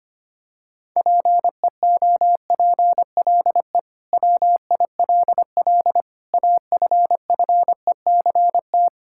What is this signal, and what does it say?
Morse code, heard as a keyed tone, about 25 wpm, 700 Hz. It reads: PEOPLE WILL AFFECT
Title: Sentences from Top 1000 Words - Review Part 6 - 25wpm